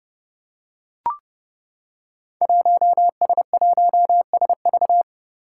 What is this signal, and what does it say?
Morse code, heard as a keyed tone, about 30 wpm, 700 Hz. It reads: 1S1SV
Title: Five Challenging Letter-Number Combinations 30wpm